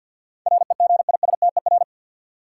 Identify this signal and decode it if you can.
Morse code, heard as a keyed tone, about 53 wpm, 700 Hz. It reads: REGISTER
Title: Sentences from Top 1200 Words - New Word Drill - 50wpm